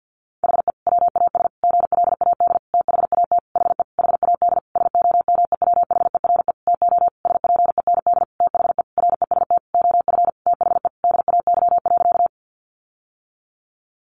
Morse code, heard as a keyed tone, about 50 wpm, 700 Hz. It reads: HE WAS GLAD THAT HE HAD SOMEWHERE TO SPEND THE REST OF THE DAY.
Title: Sentences from Top 1400 Words 50wpm